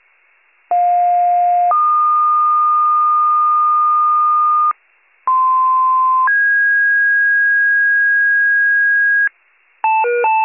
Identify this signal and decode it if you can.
Two-tone page, heard as a one-second tone, then a three-second tone, about 710 then 1220 Hz, about 1040 then 1650 Hz, with noise. It is heard from 0.7 to 9.3 s.